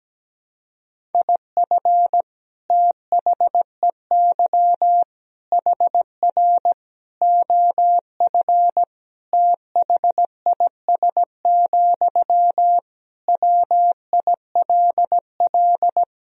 Morse code, heard as a keyed tone, about 17 wpm, 700 Hz. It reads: IF THEY HR OF THIS, WILL
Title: Sentences from Top 400 Words and Abbreviations 17wpm